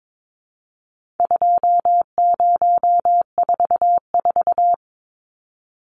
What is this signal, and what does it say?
Morse code, heard as a keyed tone, about 22 wpm, 700 Hz. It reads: 2044
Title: Numbers - 4-Digit 22wpm